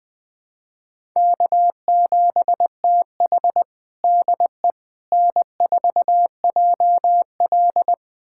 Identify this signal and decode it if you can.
Morse code, heard as a keyed tone, about 20 wpm, 700 Hz. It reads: K7TH DE N4JL
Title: USA Callsign DE Callsign - 4 Characters - 20wpm